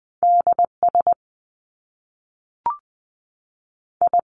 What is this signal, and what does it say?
Morse code, heard as a keyed tone, about 20 wpm, 700 Hz. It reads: D S I
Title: Top Two Letter Combinations in Words 20wpm